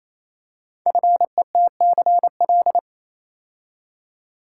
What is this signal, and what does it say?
Morse code, heard as a keyed tone, about 28 wpm, 700 Hz. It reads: FETCL
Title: Code Groups - 5 Letters - 28wpm